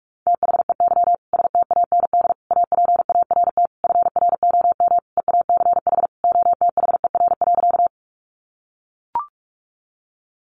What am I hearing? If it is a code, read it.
Morse code, heard as a keyed tone, about 44 wpm, 700 Hz. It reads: THEY STAND APART FROM EACH OTHER.
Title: Easy Phrases 45wpm